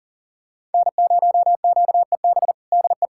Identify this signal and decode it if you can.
Morse code, heard as a keyed tone, about 37 wpm, 700 Hz. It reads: N0QEB DE
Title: USA Callsign DE Callsign 40wpm